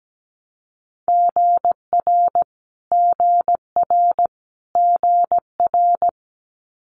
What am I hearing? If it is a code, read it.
Morse code, heard as a keyed tone, about 17 wpm, 700 Hz. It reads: GR GR GR
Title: Multi-letter Phonemes Repeated 3x 17wpm